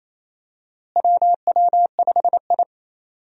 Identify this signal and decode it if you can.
Morse code, heard as a keyed tone, about 28 wpm, 700 Hz. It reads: WW5I
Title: USA Call Signs - 4 Characters - Repeated 3x - 28wpm